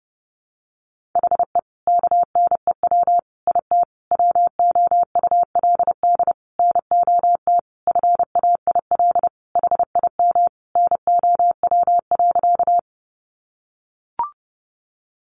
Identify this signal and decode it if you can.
Morse code, heard as a keyed tone, about 30 wpm, 700 Hz. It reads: HE KNEW IT WOULD NOT FAIL HIM NOW.
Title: Sentences from Top 1100 Words 30wpm